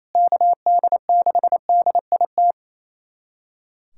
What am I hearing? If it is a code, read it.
Morse code, heard as a keyed tone, about 28 wpm, 700 Hz. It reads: KD6DIT
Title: USA Callsign DE Callsign - 6 Characters - 28wpm